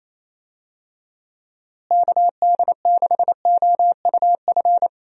Morse code, heard as a keyed tone, about 27 wpm, 700 Hz. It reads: KD6OUF
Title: USA Call Signs - 6 Characters - Repeated 3x - 28wpm